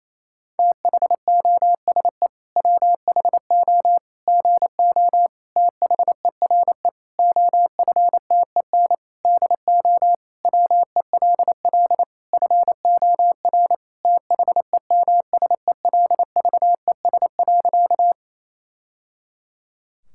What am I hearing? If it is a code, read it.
Morse code, heard as a keyed tone, about 27 wpm, 700 Hz. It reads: THOSE WHO GO THERE OFTEN DO WELL FOR THEMSELVES.
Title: Sentences from Top 1000 Words - Review Part 5 - 28wpm